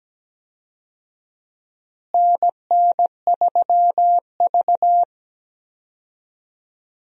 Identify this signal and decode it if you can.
Morse code, heard as a keyed tone, about 17 wpm, 700 Hz. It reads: NN3V
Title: USA Call Signs - 4 Characters - Repeated 2x - 17wpm